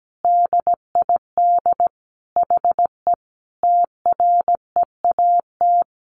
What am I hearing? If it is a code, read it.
Morse code, heard as a keyed tone, about 17 wpm, 700 Hz. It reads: DID HE TREAT
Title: Sentences from Top 1300 Words 17wpm